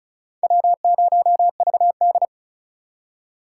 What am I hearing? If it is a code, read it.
Morse code, heard as a keyed tone, about 39 wpm, 700 Hz. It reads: W0VD
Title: Speed Racing - USA 1x2 Call Signs 35wpm